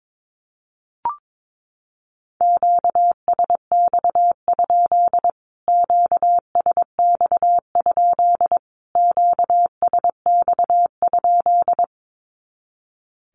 Morse code, heard as a keyed tone, about 22 wpm, 700 Hz. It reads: QSX? QSX? QSX?
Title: Q-Codes Repeated 3x 22wpm